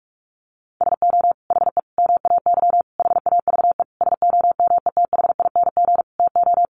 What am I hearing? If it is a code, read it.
Morse code, heard as a keyed tone, about 44 wpm, 700 Hz. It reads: SO HE MAY HAVE SOMETHING TO
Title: Sentences from Top 1500 Words 45wpm